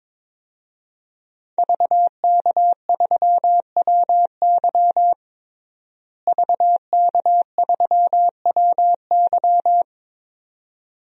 Morse code, heard as a keyed tone, about 22 wpm, 700 Hz. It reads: VK3WY VK3WY
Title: DXCC-100 Call Signs - 5 Characters - Repeated 2x - 22wpm